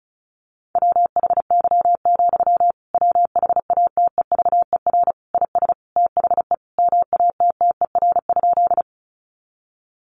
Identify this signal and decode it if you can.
Morse code, heard as a keyed tone, about 34 wpm, 700 Hz. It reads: WHY, WHATEVER IS THE MATTER?
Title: Sentences from Top 1100 Words 35wpm